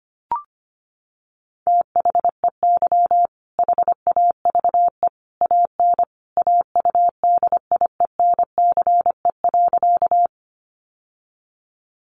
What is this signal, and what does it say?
Morse code, heard as a keyed tone, about 25 wpm, 700 Hz. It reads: THEY HAVE AN AUDIENCE.
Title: Sentences from Top 800 Words 25wpm